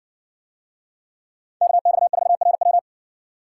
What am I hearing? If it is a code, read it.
Morse code, heard as a keyed tone, about 60 wpm, 700 Hz. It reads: KX4RW